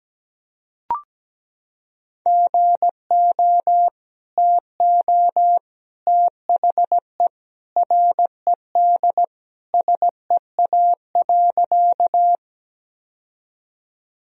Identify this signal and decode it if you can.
Morse code, heard as a keyed tone, about 17 wpm, 700 Hz. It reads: GO TO THE RED SEA.
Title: Sentences from Top 1000 Words - Review Part 2 - 17wpm